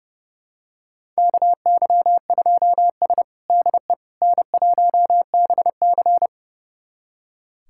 Morse code, heard as a keyed tone, about 30 wpm, 700 Hz. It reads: KY2S DE N1BC
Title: USA Callsign DE Callsign - 4 Characters - 30wpm